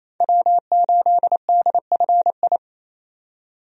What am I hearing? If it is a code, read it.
Morse code, heard as a keyed tone, about 28 wpm, 700 Hz. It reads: W8DFI
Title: USA Call Signs - 5 Characters - 28wpm